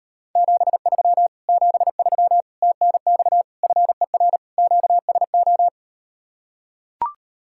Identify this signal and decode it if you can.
Morse code, heard as a keyed tone, about 38 wpm, 690 Hz. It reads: 73 73 TNX FER QSO